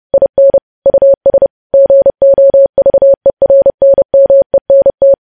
Morse code, heard as a keyed tone, about 30 wpm, 550 Hz. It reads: IN US GOVERNMENT
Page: news headlines in morse code at 30 wpm